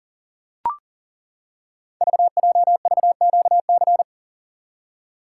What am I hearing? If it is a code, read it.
Morse code, heard as a keyed tone, about 38 wpm, 690 Hz. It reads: VJVQC
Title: Code Groups - 5 Letters - 40wpm